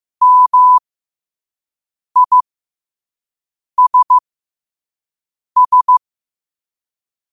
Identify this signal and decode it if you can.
Morse code, heard as a keyed tone, about 15 wpm, spaced as for 5 wpm, 1000 Hz. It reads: MISS
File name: 5wpm